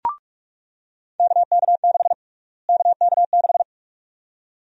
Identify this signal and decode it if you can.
Morse code, heard as a keyed tone, about 42 wpm, 700 Hz. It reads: KK6 KK6
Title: USA Call Sign Prefixes - Repeated 2x - 45wpm